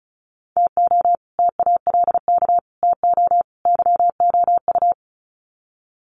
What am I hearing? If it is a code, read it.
Morse code, heard as a keyed tone, about 34 wpm, 700 Hz. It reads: TO TALK TO YOU
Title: Sets of 4 Words 35wpm